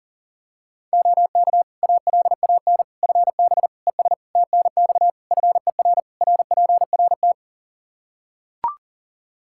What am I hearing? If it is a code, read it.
Morse code, heard as a keyed tone, about 37 wpm, 700 Hz. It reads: OK ALAN FB ES TNX FER RPRT
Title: Practice QSO - Exchange 2 - Info - 40wpm